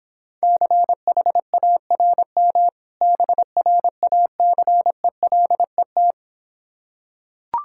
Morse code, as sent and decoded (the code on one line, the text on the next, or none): -.-. .... .- .-. -- / -... .-. .- -.-. .-.. -
CHARM BRACELET